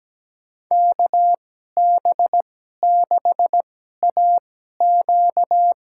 Morse code, heard as a keyed tone, about 17 wpm, spaced as for 12 wpm, 700 Hz. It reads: KB6AQ
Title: USA Call Signs - 2x Character Spacing - 17wpm